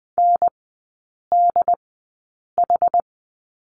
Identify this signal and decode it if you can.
Morse code, heard as a keyed tone, about 20 wpm, spaced as for 8 wpm, 700 Hz. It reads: NDH
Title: 107 Lesson 18e - Letter m - Three Characters 20wpm